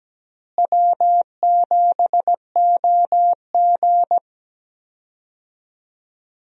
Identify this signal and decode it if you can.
Morse code, heard as a keyed tone, about 17 wpm, 700 Hz. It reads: W7OG
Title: USA Call Signs - 4 Characters - Repeated 3x - 17wpm